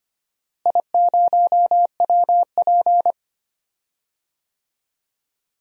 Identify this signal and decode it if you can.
Morse code, heard as a keyed tone, about 25 wpm, 700 Hz. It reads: I0WP
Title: DXCC-100 Call Signs - 4 Characters - Repeated 3x - 25wpm